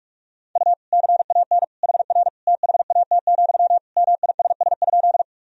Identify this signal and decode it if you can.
Morse code, heard as a keyed tone, about 46 wpm, 700 Hz. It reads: U CAN HR THAT, MISS?
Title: Sentences from Top 500 Words and Abbreviations 45wpm